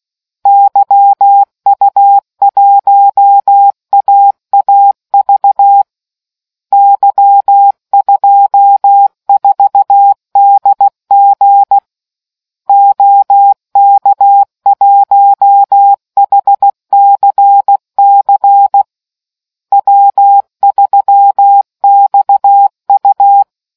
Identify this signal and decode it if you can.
Morse code, heard as a keyed tone, about 16 wpm, 800 Hz. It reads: YU1AAV Y24DG OK1HCC W3XU